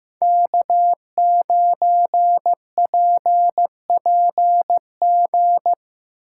Morse code, heard as a keyed tone, about 15 wpm, 700 Hz. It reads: K9PPG
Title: USA Call Signs - 5 Characters - 6x Character Spacing - 15wpm